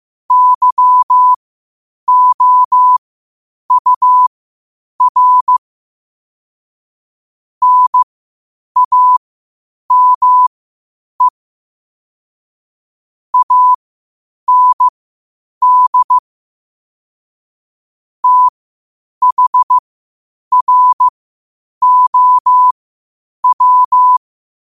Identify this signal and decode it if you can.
Morse code, heard as a keyed tone, about 15 wpm, spaced as for 8 wpm, 1000 Hz. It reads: YOUR NAME AND THROW
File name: qotdfiles / 8wpm